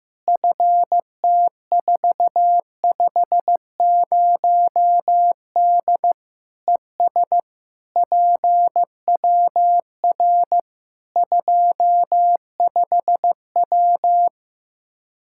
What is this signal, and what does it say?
Morse code, heard as a keyed tone, about 15 wpm, 700 Hz. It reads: FT450D ES PWR 25W
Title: Practice QSO - Exchange 2 - Info - 15wpm